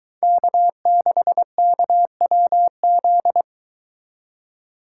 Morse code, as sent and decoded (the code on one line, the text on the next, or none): -.- -.... -.- .-- --..
K6KWZ